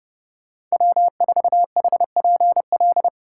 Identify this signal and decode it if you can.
Morse code, heard as a keyed tone, about 35 wpm, 700 Hz. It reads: W4HPL
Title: Speed Racing - USA 1x3 Call Signs 30wpm